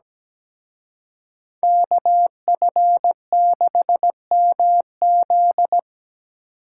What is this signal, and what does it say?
Morse code, heard as a keyed tone, about 20 wpm, 700 Hz. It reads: KF6MZ
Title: Speed Racing - USA 2x2 Call Signs 17wpm